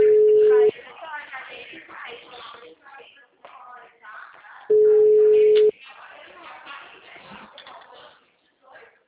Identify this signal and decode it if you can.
Ringback tone (Europe, 425 Hz), with speech.